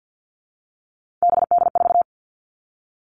Morse code, heard as a keyed tone, about 50 wpm, 700 Hz. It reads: BD4
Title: All Cut Numbers - 3-Digit 50wpm